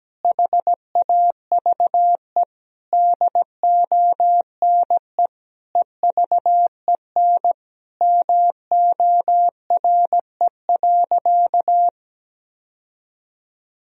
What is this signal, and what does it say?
Morse code, heard as a keyed tone, about 17 wpm, 700 Hz. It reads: HAVE DONE EVEN MORE.
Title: Sentences from Top 1000 Words - Review Part 1 - 17wpm